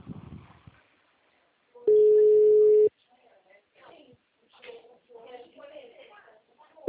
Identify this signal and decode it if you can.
Ringback tone (Europe, 425 Hz), with speech.